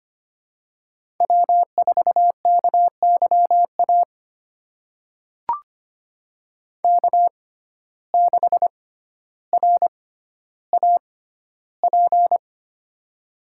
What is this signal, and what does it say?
Morse code, heard as a keyed tone, about 25 wpm, spaced as for 13 wpm, 700 Hz. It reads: W4KYA K6RAP
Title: USA Call Signs - 5 Characters - 6x Character Spacing - 25wpm